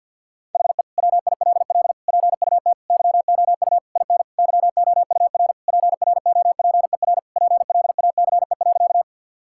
Morse code, heard as a keyed tone, about 50 wpm, 700 Hz. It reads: HE WILL PUT YOU IN YOUR PROPER PLACE.
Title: Sentences from Top 1000 Words - Review Part 5 - 50wpm